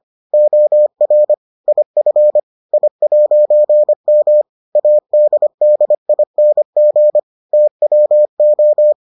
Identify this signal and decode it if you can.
Morse code, heard as a keyed tone, about 25 wpm, 600 Hz. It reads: OR IF I'M ADDING TWO